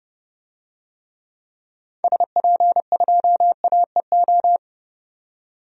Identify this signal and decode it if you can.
Morse code, heard as a keyed tone, about 30 wpm, 700 Hz. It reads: SP2AEO
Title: DXCC-100 Call Signs - 6 Characters - 30wpm